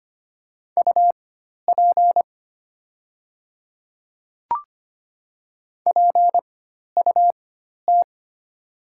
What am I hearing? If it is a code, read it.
Morse code, heard as a keyed tone, about 25 wpm, spaced as for 12 wpm, 700 Hz. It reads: UP PUT
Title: 4x Character Spacing - Top 300 Words - 25wpm